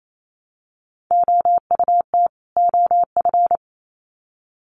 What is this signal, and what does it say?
Morse code, heard as a keyed tone, about 28 wpm, 700 Hz. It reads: OUT OF
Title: Sets of 2 Words 28wpm